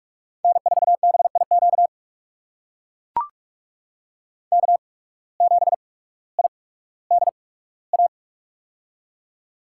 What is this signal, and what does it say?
Morse code, heard as a keyed tone, about 41 wpm, spaced as for 19 wpm, 700 Hz. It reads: N4BIQ K7IDA
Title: USA Call Signs - 5 Characters - 8x Character Spacing - 45wpm